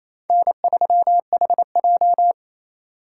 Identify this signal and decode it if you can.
Morse code, heard as a keyed tone, about 28 wpm, 700 Hz. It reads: N3HJ
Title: USA Call Signs - 4 Characters - 2x Character Spacing - 28wpm